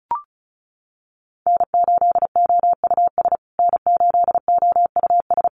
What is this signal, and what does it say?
Morse code, heard as a keyed tone, about 34 wpm, 700 Hz. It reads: N8OUS N8OUS
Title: USA Call Signs Repeated 3x 35wpm